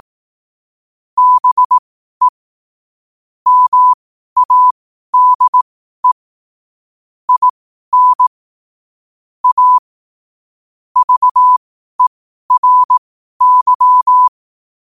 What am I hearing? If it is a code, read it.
Morse code, heard as a keyed tone, about 18 wpm, spaced as for 12 wpm, 1000 Hz. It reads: BE MADE IN A VERY